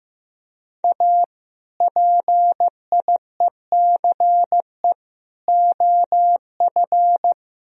Morse code, heard as a keyed tone, about 17 wpm, 700 Hz. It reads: A PIECE OF
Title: Speed-Racing - Sets of 3 Words 15wpm